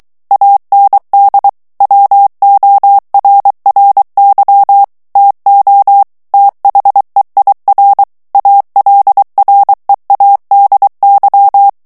Morse code, heard as a keyed tone, about 23 wpm, 800 Hz. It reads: AND WORRY TO THEIR ALREADY
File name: morse mp3 / ANGLAIS texte clair 1 vitesse 22 mots minute